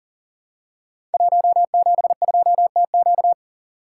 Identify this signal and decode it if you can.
Morse code, heard as a keyed tone, about 40 wpm, 700 Hz. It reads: 172TQ
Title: Code Groups - 5 Letter:Numbers - 40wpm